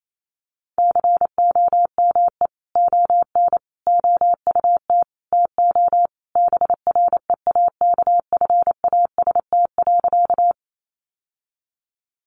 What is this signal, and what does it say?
Morse code, heard as a keyed tone, about 28 wpm, 700 Hz. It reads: COME ON OUT TO BREAKFAST.
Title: Sentences from Top 1500 Words 28wpm